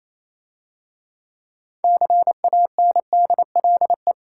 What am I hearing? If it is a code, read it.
Morse code, heard as a keyed tone, about 27 wpm, 690 Hz. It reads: CANDLE
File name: Easy Words - Speed-Racing with Hints - 22wpm